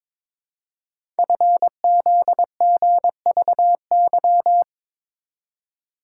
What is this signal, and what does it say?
Morse code, heard as a keyed tone, about 23 wpm, 700 Hz. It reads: FZGVY
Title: Code Groups - 5 Letters - 22wpm